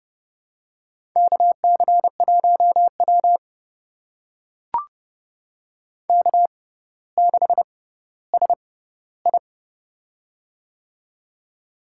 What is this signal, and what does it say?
Morse code, heard as a keyed tone, about 30 wpm, spaced as for 15 wpm, 700 Hz. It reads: KC1W K6SI
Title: USA Call Signs - 4 Characters - 6x Character Spacing - 30wpm